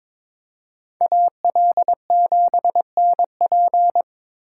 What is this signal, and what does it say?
Morse code, heard as a keyed tone, about 26 wpm, 700 Hz. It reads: AL7NP
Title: Speed Racing - USA Call Signs 22wpm